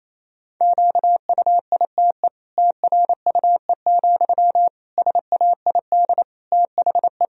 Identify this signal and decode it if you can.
Morse code, heard as a keyed tone, about 28 wpm, 700 Hz. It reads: QUITE TRUE, SAID THE
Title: Sentences from Top 1000 Words - Review Part 3 - 28wpm